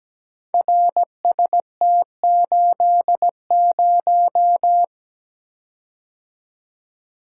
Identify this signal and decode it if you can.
Morse code, heard as a keyed tone, about 17 wpm, 700 Hz. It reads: RST80
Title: Code Groups - 5 Letter:Numbers - 17wpm